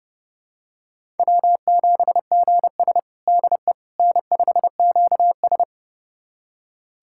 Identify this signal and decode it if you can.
Morse code, heard as a keyed tone, about 30 wpm, 700 Hz. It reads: W7GS DE N5QS
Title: USA Callsign DE Callsign - 4 Characters - 30wpm